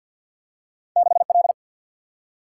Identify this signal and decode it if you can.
Morse code, heard as a keyed tone, about 47 wpm, 700 Hz. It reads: 6 L
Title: Single-Letter Single-Number 50wpm